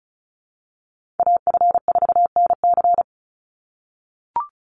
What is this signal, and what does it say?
Morse code, heard as a keyed tone, about 34 wpm, 700 Hz. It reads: AF4NC
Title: Call Signs - 35wpm